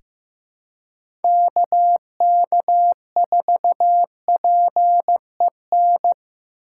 Morse code, heard as a keyed tone, about 15 wpm, 700 Hz. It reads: KK4PEN
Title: USA Call Signs - 3x Character Spacing - 15wpm